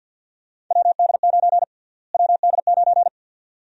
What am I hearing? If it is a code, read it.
Morse code, heard as a keyed tone, about 50 wpm, 700 Hz. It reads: WD9 WD9
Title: USA Call Sign Prefixes - Repeated 2x - 50wpm